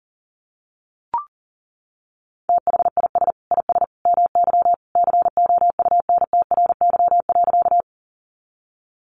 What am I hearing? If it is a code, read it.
Morse code, heard as a keyed tone, about 40 wpm, 700 Hz. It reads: THIS IS MY COUNTRY.
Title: Sentences from Top 300 Words 40wpm